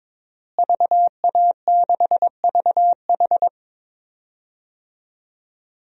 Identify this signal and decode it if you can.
Morse code, heard as a keyed tone, about 22 wpm, 700 Hz. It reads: VA6VH
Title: DXCC-100 Call Signs - 5 Characters - Repeated 3x - 22wpm